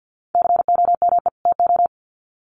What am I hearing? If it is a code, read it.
Morse code, heard as a keyed tone, about 50 wpm, 700 Hz. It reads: COME TO
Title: Sets of 2 Words 50wpm